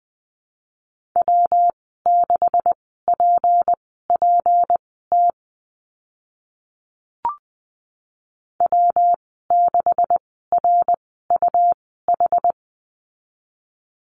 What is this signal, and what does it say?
Morse code, heard as a keyed tone, about 20 wpm, spaced as for 14 wpm, 700 Hz. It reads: W6PPT W6RUH
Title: 193 Lesson 31g - Letter v - Call Signs 20wpm